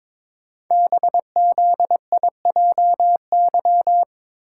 Morse code, heard as a keyed tone, about 22 wpm, 700 Hz. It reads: BZIJY
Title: Code Groups - 5 Letters - 22wpm